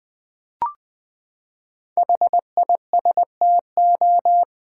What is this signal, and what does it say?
Morse code, heard as a keyed tone, about 20 wpm, 700 Hz. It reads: HISTO
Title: Build A Word - 5 Letters At A Time - 10-Letter Words - 20wpm